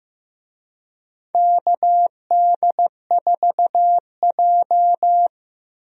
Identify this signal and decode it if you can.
Morse code, heard as a keyed tone, about 16 wpm, 700 Hz. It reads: KD4J
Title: USA Call Signs - 4 Characters - Repeated 2x - 15wpm